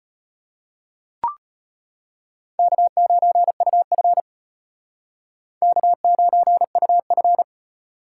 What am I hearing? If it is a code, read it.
Morse code, heard as a keyed tone, about 32 wpm, 700 Hz. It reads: K9UF K9UF